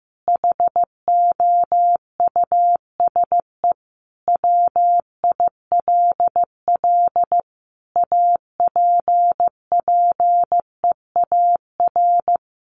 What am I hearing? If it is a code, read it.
Morse code, heard as a keyed tone, about 14 wpm, 700 Hz. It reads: HOUSE WILL APPEAR
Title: Sentences from Top 1500 Words 15wpm